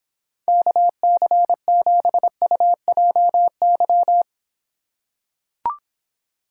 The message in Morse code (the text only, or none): KC7UJY